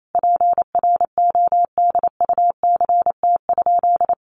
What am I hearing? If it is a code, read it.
Morse code, heard as a keyed tone, about 28 wpm, 700 Hz. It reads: PRODUCT?
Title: Sentences from Top 1400 Words 28wpm